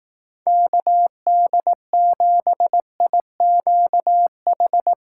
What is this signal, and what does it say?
Morse code, heard as a keyed tone, about 18 wpm, 700 Hz. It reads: KD7IQH
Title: Speed Racing - USA 2x3 Call Signs 15wpm